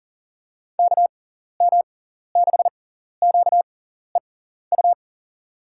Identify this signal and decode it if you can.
Morse code, heard as a keyed tone, about 40 wpm, spaced as for 14 wpm, 700 Hz. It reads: KM6QEU
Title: USA Call Signs - 6 Characters - 6x Character Spacing - 40wpm